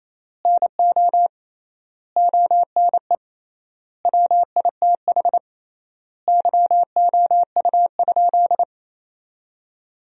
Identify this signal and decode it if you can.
Morse code, heard as a keyed tone, about 28 wpm, 700 Hz. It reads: NO ONE WITH YOU?
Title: 3x Word-Spacing - Sentences from Top 500 Words - 28wpm